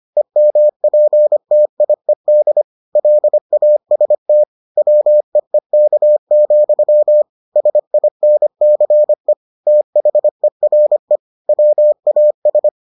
Morse code, heard as a keyed tone, about 25 wpm, 600 Hz. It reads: EMPTIED LAST WEEK, SINCE THERE WAS